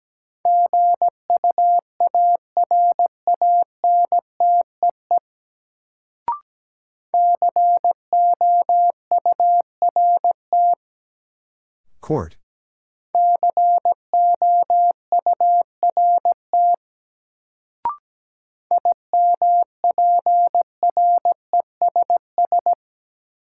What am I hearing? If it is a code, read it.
Morse code, heard as a keyed tone, about 17 wpm, 700 Hz. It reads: GUARANTEE COURT COURT IMPRESS
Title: Sentences from Top 1600 Words - New Word Drill - 17wpm